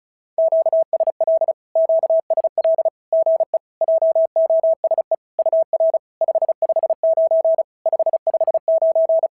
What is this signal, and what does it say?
Morse code, heard as a keyed tone, about 34 wpm, 650 Hz. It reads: QSL QSL GE JOSE UR 559 559
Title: SOTA QSO Practice 35wpm